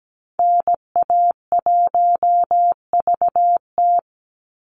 Morse code, heard as a keyed tone, about 17 wpm, 700 Hz. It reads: NA1VT
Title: Call Signs Repeated 3x - 17wpm